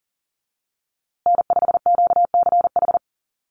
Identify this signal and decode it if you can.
Morse code, heard as a keyed tone, about 39 wpm, 700 Hz. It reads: N5QCH
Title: USA Call Signs Repeated 3x 40wpm